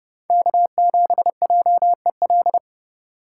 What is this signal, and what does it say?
Morse code, heard as a keyed tone, about 30 wpm, 700 Hz. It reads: K7JEL